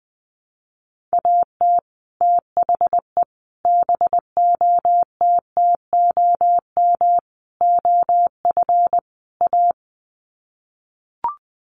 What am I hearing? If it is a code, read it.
Morse code, heard as a keyed tone, about 20 wpm, 700 Hz. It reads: AT THE BOTTOM OF A
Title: Sets of 5 Words Encore2 20wpm